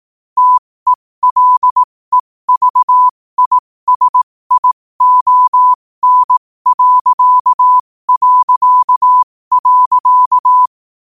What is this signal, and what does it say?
Morse code, heard as a keyed tone, about 18 wpm, spaced as for 15 wpm, 1000 Hz. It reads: TELEVISION...
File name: 15wpm